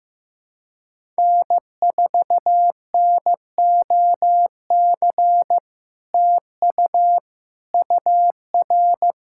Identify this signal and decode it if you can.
Morse code, heard as a keyed tone, about 16 wpm, 700 Hz. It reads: N4NOC TU UR
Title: SOTA QSO Practice 15wpm